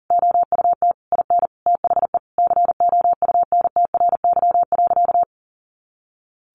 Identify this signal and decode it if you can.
Morse code, heard as a keyed tone, about 37 wpm, 700 Hz. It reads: OUT IN THE COUNTRY.
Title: Sentences from Top 700 Words 40wpm